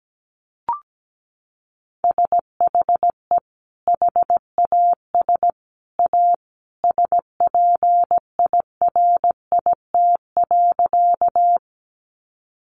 Morse code, heard as a keyed tone, about 17 wpm, 700 Hz. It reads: SHE HAS A SPIRIT.
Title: Sentences from Top 1400 Words 17wpm